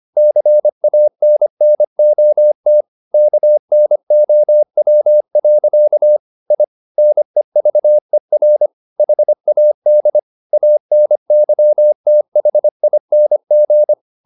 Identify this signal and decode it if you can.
Morse code, heard as a keyed tone, about 25 wpm, 600 Hz. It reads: CANNOT KNOW. I NEVER HAD ANYTHING